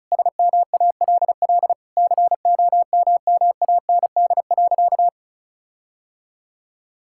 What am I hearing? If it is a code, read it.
Morse code, heard as a keyed tone, about 36 wpm, 700 Hz. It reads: SMALL COMMAND.
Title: Sentences from Top 1600 Words 35wpm